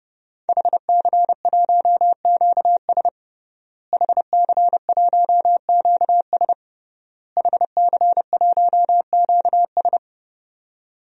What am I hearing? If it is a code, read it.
Morse code, heard as a keyed tone, about 30 wpm, 700 Hz. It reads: HC1QS HC1QS HC1QS
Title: DXCC-100 Call Signs - 5 Characters - Repeated 3x - 30wpm